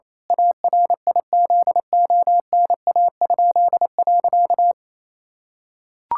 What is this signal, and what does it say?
Morse code, heard as a keyed tone, about 28 wpm, 700 Hz. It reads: ARIZONA?.
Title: Pangram Challenge v2 - 28wpm